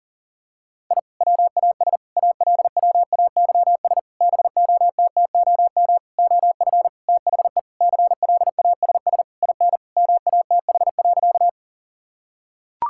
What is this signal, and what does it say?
Morse code, heard as a keyed tone, about 38 wpm, 700 Hz. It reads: I WAS ALWAYS BOTTOM OF THE CLASS IN MATH.
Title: Sentences from Top 1000 Words - Review Part 6 - 40wpm